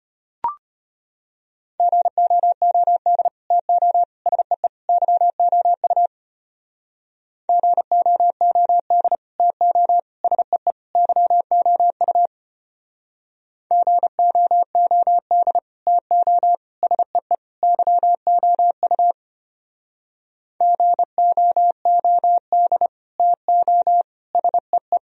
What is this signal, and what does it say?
Morse code, heard as a keyed tone, about 32 wpm, 700 Hz. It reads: GOOD TO SEE YOU GOOD TO SEE YOU GOOD TO SEE YOU GOOD TO SEE